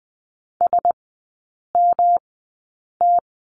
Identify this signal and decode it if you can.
Morse code, heard as a keyed tone, about 20 wpm, spaced as for 8 wpm, 700 Hz. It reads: SMT
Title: 107 Lesson 18e - Letter m - Three Characters 20wpm